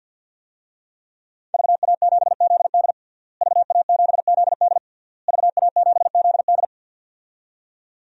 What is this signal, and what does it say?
Morse code, heard as a keyed tone, about 52 wpm, 700 Hz. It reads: VA7ZD VA7ZD VA7ZD
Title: DXCC-100 Call Signs - 5 Characters - Repeated 3x - 50wpm